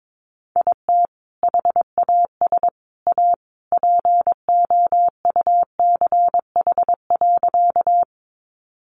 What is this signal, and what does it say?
Morse code, heard as a keyed tone, about 22 wpm, 700 Hz. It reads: IT HAS A POUCH.
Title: Easy Phrases 22wpm